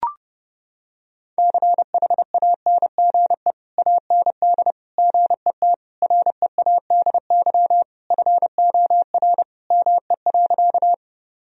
Morse code, heard as a keyed tone, about 30 wpm, 700 Hz. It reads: CHANGE AND GET READY FOR ME.
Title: Sentences from Top 1000 Words - Review Part 1 - 30wpm